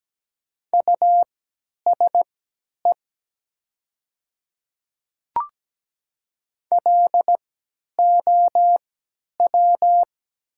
Morse code, heard as a keyed tone, about 17 wpm, spaced as for 10 wpm, 700 Hz. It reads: USE LOW
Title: 3x Character Spacing - Top 300 Words - 17wpm